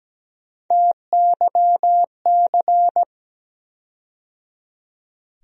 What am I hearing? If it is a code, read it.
Morse code, heard as a keyed tone, about 17 wpm, 700 Hz. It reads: TYC
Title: USA Call Sign Postfixes - 3 Letters 17wpm